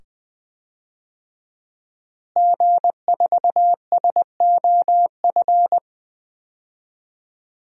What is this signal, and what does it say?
Morse code, heard as a keyed tone, about 20 wpm, 700 Hz. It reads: G4SOF
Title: DXCC-100 Call Signs - 5 Characters - Repeated 2x - 20wpm